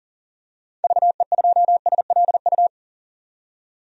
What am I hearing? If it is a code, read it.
Morse code, heard as a keyed tone, about 40 wpm, 700 Hz. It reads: VE2SLU
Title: DXCC-100 Call Signs - 6 Characters - 40wpm